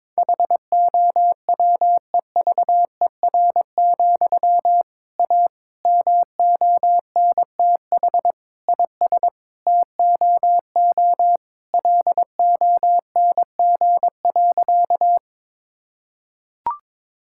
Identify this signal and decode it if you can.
Morse code, heard as a keyed tone, about 22 wpm, 700 Hz. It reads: HOWEVER, A MONTH IS TOO LONG.
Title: Sentences from Top 1000 Words - Review Part 5 - 22wpm